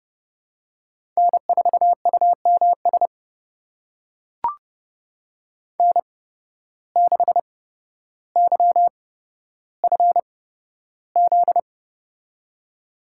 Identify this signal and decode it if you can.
Morse code, heard as a keyed tone, about 30 wpm, spaced as for 13 wpm, 700 Hz. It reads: N4UMS N6YFZ
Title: USA Call Signs - 8x Character Spacing - 30wpm